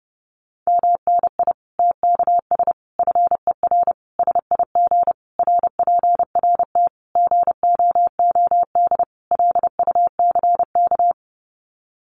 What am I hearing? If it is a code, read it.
Morse code, heard as a keyed tone, about 30 wpm, 700 Hz. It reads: MNI TKS FER SIG RPRT GOOD LUCK
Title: Common QSO Phrases 30wpm